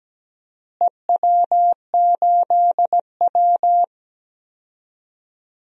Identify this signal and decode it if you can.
Morse code, heard as a keyed tone, about 17 wpm, 700 Hz. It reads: EW8W
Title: DXCC-100 Call Signs - 4 Characters - Repeated 3x - 17wpm